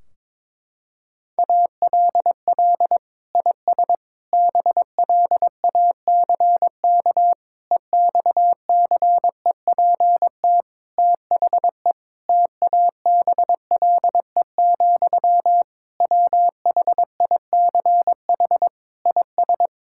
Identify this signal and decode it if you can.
Morse code, heard as a keyed tone, about 21 wpm, 700 Hz. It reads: ALL IS BLACK EXCEPT THE TABLE, WHICH IS
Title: Sentences from Top 1600 Words 22wpm